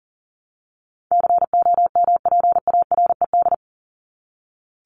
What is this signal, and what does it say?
Morse code, heard as a keyed tone, about 40 wpm, 700 Hz. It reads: COMPARED
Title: ED Suffix - Top 100 Words with Base Words 40wpm